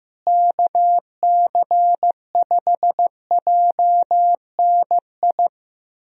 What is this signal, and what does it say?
Morse code, heard as a keyed tone, about 15 wpm, 700 Hz. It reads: KC5JNI
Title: USA Call Signs - 6 Characters - Repeated 3x - 15wpm